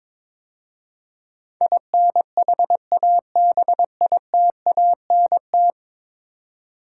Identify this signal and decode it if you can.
Morse code, heard as a keyed tone, about 21 wpm, 700 Hz. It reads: INHABITANT
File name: practice / 22wpm / 10 Letter Words - First 3 Letters Given - 22wpm